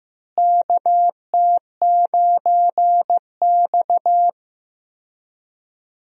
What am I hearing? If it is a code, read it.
Morse code, heard as a keyed tone, about 15 wpm, 700 Hz. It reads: KT9X
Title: USA Call Signs - 4 Characters - Repeated 2x - 15wpm